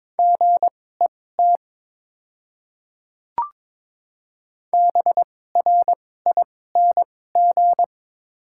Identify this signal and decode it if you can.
Morse code, heard as a keyed tone, about 22 wpm, spaced as for 16 wpm, 700 Hz. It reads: GET BRING
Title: 2x Character Spacing - Top 500 Words - 22wpm